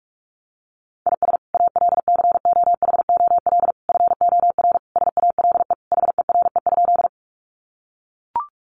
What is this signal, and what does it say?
Morse code, heard as a keyed tone, about 44 wpm, 700 Hz. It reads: IS ALCOHOL FOR SALE HERE?
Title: Sentences from Top 1400 Words 45wpm